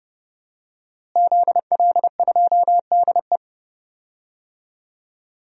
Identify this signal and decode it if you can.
Morse code, heard as a keyed tone, about 30 wpm, 700 Hz. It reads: ZL2DE
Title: DXCC-100 Call Signs - 5 Characters - Repeated 3x - 30wpm